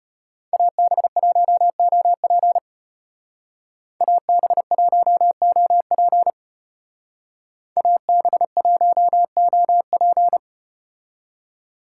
Morse code, heard as a keyed tone, about 32 wpm, 700 Hz. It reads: AB1OP AB1OP AB1OP